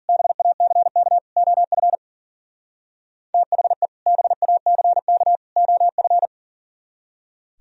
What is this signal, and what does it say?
Morse code, heard as a keyed tone, about 50 wpm, 700 Hz. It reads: BACK OF THE BACK OF